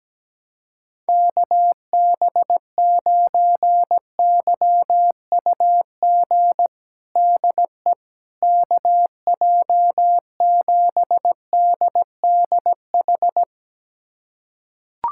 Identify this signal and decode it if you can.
Morse code, heard as a keyed tone, about 17 wpm, 700 Hz. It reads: KB9YUG DE KJ7DDH
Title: USA Callsign DE Callsign - 6 Characters - 17wpm